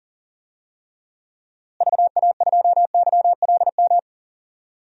Morse code, heard as a keyed tone, about 40 wpm, 700 Hz. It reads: VA2YLM
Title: DXCC-100 Call Signs - 6 Characters - Repeated 2x - 40wpm